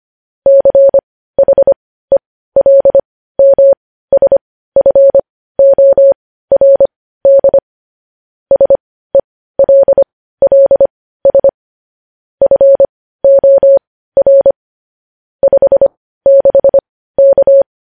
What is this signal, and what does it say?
Morse code, heard as a keyed tone, about 25 wpm, spaced as for 15 wpm, 550 Hz. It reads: CHELMSFORD SELLS FOR 56K
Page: News Headlines in Morse Code at 15 WPM